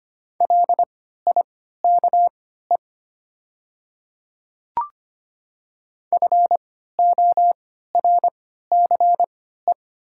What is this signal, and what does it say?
Morse code, heard as a keyed tone, about 25 wpm, spaced as for 14 wpm, 690 Hz. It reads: LIKE FORCE
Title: 3x Character Spacing - Top 500 Words - 25wpm